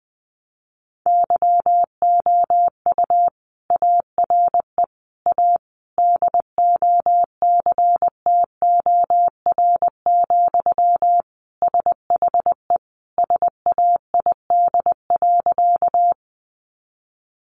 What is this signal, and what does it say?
Morse code, heard as a keyed tone, about 20 wpm, 700 Hz. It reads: YOU ARE A DOCTOR, SHE SAID.
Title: Sentences from Top 1500 Words 20wpm